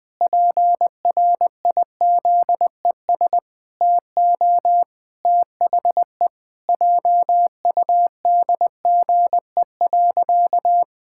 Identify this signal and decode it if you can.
Morse code, heard as a keyed tone, about 19 wpm, 700 Hz. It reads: PRIZES TO THE JUDGE.
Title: Pangram Challenge v2 - 20wpm